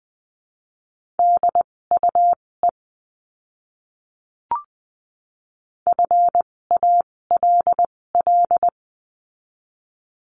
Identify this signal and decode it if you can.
Morse code, heard as a keyed tone, about 20 wpm, spaced as for 16 wpm, 700 Hz. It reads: DUE FALL
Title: 256 Lesson 42a - Words 12-20wpm